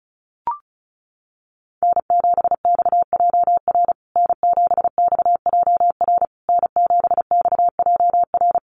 Morse code, heard as a keyed tone, about 34 wpm, 700 Hz. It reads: N7XJR N7XJR N7XJR
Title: USA Call Signs Repeated 3x 35wpm